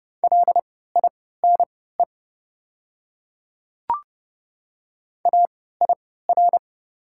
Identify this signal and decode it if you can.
Morse code, heard as a keyed tone, about 30 wpm, spaced as for 17 wpm, 700 Hz. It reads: LINE AIR
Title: 3x Character Spacing - Top 300 Words - 30wpm